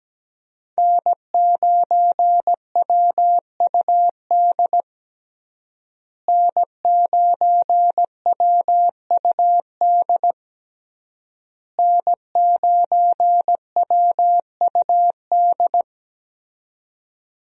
Morse code, heard as a keyed tone, about 17 wpm, 700 Hz. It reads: N9WUD N9WUD N9WUD
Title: USA Call Signs - 5 Characters - Repeated 3x - 17wpm